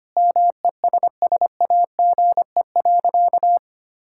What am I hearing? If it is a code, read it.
Morse code, heard as a keyed tone, about 25 wpm, 700 Hz. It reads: MESSAGE.
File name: practice / 25wpm / Sentences with 9-Word Phrases - 25wpm